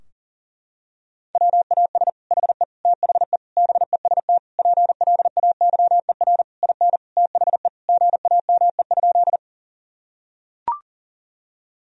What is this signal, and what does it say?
Morse code, heard as a keyed tone, about 40 wpm, 700 Hz. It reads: WAS HE THE BEST PLAYER IN THE GAME?
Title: Sentences from Top 1000 Words - Review Part 4 - 40wpm